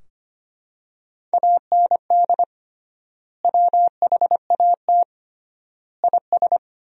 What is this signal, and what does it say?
Morse code, heard as a keyed tone, about 25 wpm, 700 Hz. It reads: AND WHAT IS
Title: 3x Word-Spacing - Sentences from Top 100 Words - 25wpm